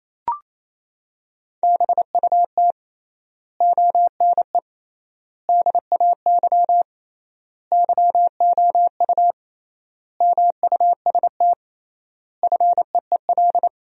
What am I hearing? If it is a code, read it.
Morse code, heard as a keyed tone, about 28 wpm, 700 Hz. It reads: BUT ONE DAY YOU MUST FEEL
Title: 3x Word-Spacing - Sentences from Top 400 Words - 28wpm